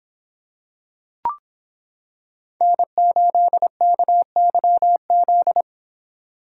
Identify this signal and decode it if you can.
Morse code, heard as a keyed tone, about 25 wpm, 700 Hz. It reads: N8KYZ